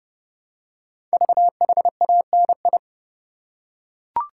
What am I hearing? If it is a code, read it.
Morse code, heard as a keyed tone, about 31 wpm, 690 Hz. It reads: VHANI
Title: Code Groups - 5 Letters - 30wpm